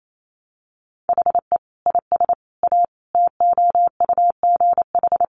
Morse code, heard as a keyed tone, about 28 wpm, 700 Hz. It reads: HE IS A TOUGH
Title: Sentences from Top 1200 Words 28wpm